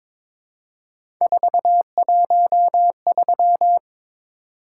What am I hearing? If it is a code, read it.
Morse code, heard as a keyed tone, about 22 wpm, 700 Hz. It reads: 413
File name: practice / 22wpm / Three Letter-Number Combinations - 22wpm